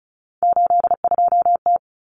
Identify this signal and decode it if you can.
Morse code, heard as a keyed tone, about 34 wpm, 700 Hz. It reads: 82T
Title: Common Cut Numbers - 3-Digit 35wpm